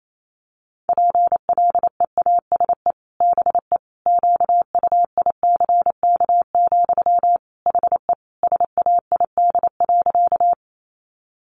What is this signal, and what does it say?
Morse code, heard as a keyed tone, about 28 wpm, 700 Hz. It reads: PLEASE BE QUICK, HE SAID.
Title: Sentences from Top 1500 Words 28wpm